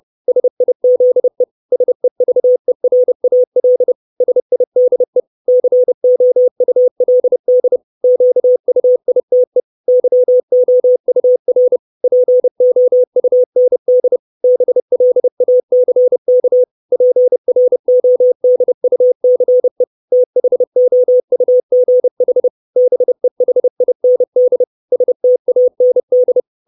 Morse code, heard as a keyed tone, about 30 wpm, 500 Hz. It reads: SIZE SEVERAL SIDE COULD QUITE YOUR POUND BLACK PRODUCE THOUGH BEHIND STAND
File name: 632 Words 30WPM